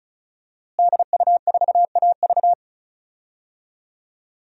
Morse code, heard as a keyed tone, about 36 wpm, 700 Hz. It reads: DU4AV
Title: DXCC-100 Call Signs - 5 Characters - Repeated 2x - 35wpm